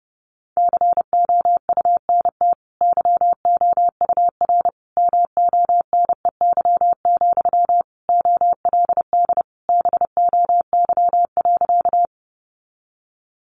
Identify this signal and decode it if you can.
Morse code, heard as a keyed tone, about 30 wpm, 700 Hz. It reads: COUNT YOUR MONEY, OLD BOY.
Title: Sentences from Top 1500 Words 30wpm